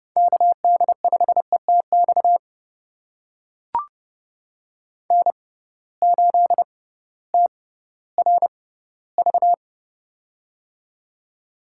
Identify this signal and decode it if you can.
Morse code, heard as a keyed tone, about 30 wpm, spaced as for 16 wpm, 700 Hz. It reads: KD5ETX N8TRV
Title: USA Call Signs - 6x Character Spacing - 30wpm